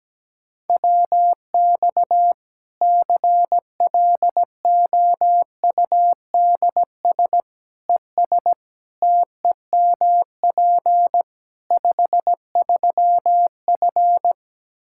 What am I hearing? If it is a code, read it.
Morse code, heard as a keyed tone, about 17 wpm, 700 Hz. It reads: WX CLOUDS ES TEMP 53F
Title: Practice QSO - Exchange 2 - Info - 17wpm